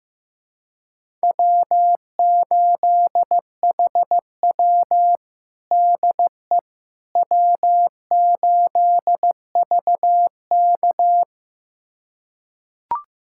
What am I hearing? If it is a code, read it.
Morse code, heard as a keyed tone, about 15 wpm, 700 Hz. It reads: W8HW DE W8VK
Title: USA Callsign DE Callsign - 4 Characters - 15wpm